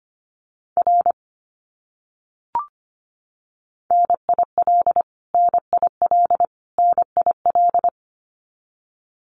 Morse code, heard as a keyed tone, about 25 wpm, 700 Hz. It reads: R NIL NIL NIL
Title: Abbreviations Repeated 3x - 25wpm